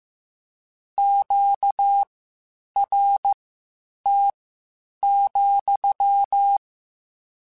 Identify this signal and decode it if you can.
Morse code, heard as a keyed tone, about 15 wpm, spaced as for 8 wpm, 800 Hz. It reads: QRT,
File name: Q Calls / Q Calls-mp3s / Q Calls sppc @13